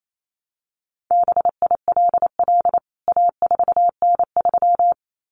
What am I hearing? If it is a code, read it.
Morse code, heard as a keyed tone, about 28 wpm, 700 Hz. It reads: BILL A4N3
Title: CWT Exchange 28wpm